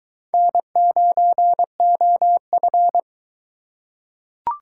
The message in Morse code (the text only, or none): N9OF